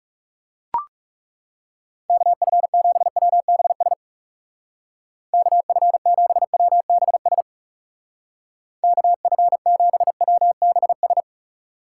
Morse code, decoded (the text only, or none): KF7WBS KF7WBS KF7WBS